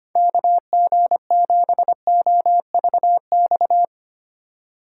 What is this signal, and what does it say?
Morse code, heard as a keyed tone, about 31 wpm, 700 Hz. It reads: KG7OVX
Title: Speed Racing - USA 2x3 Call Signs 25wpm